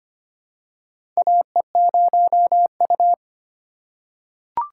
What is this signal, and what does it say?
Morse code, heard as a keyed tone, about 25 wpm, 700 Hz. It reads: AE0U
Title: USA Call Signs - 4 Characters - 3x Character Spacing - 25wpm